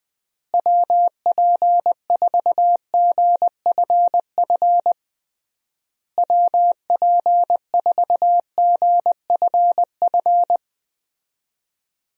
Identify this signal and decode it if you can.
Morse code, heard as a keyed tone, about 21 wpm, 700 Hz. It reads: WP4GFF WP4GFF
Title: USA Call Signs - 6 Characters - Repeated 3x - 20wpm